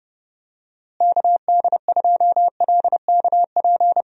Morse code, heard as a keyed tone, about 31 wpm, 700 Hz. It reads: KD2LKP
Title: USA Call Signs - 6 Characters - Repeated 3x - 30wpm